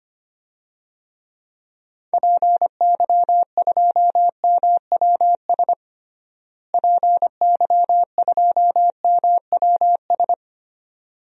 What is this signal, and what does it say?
Morse code, heard as a keyed tone, about 25 wpm, 700 Hz. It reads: PY2MWS PY2MWS
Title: DXCC-100 Call Signs - 6 Characters - Repeated 2x - 25wpm